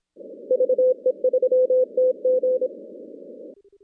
Morse code, heard as a keyed tone, about 26 wpm, 510 Hz. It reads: VE3TG
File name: contests / CW-SS-k5tr- -2100